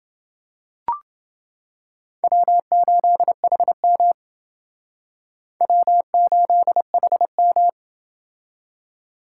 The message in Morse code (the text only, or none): W8HM W8HM